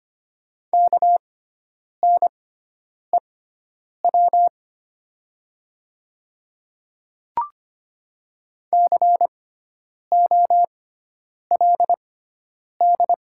Morse code, heard as a keyed tone, about 25 wpm, spaced as for 9 wpm, 700 Hz. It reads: KNEW COLD
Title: 6x Character Spacing - Top 500 Words - 25wpm